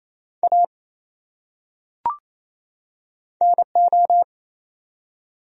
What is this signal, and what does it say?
Morse code, heard as a keyed tone, about 28 wpm, 700 Hz. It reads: A NO
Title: Top 100 Words 28wpm